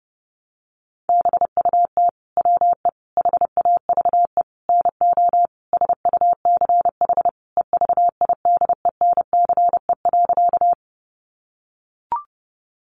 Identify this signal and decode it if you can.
Morse code, heard as a keyed tone, about 30 wpm, 700 Hz. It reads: BUT WE HAVE NO SUCH EVIDENCE.
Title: Sentences from Top 1300 Words 30wpm